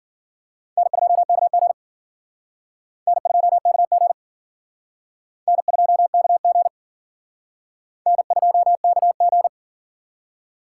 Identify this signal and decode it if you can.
Morse code, heard as a keyed tone, about 52 wpm, 700 Hz. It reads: N2KG N2KG N2KG N2KG